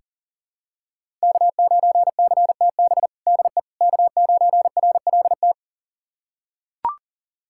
Morse code, heard as a keyed tone, about 40 wpm, 700 Hz. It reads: K9CTB DE K9RLT
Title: USA Callsign DE Callsign - 5 Characters - 40wpm